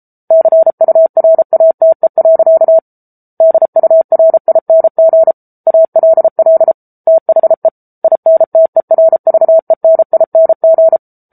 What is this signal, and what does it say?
Morse code, heard as a keyed tone, about 34 wpm, 650 Hz. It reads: CURATE. DURING ALL THE INTERVENING